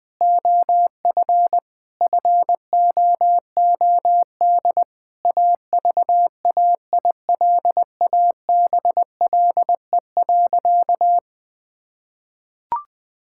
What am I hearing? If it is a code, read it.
Morse code, heard as a keyed tone, about 20 wpm, 700 Hz. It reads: OF FOOD AVAILABLE.
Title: Sentences from Top 1000 Words - Review Part 6 - 20wpm